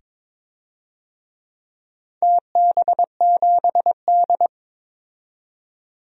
Morse code, heard as a keyed tone, about 22 wpm, 700 Hz. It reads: TB7D
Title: DXCC-100 Call Signs - 4 Characters - 22wpm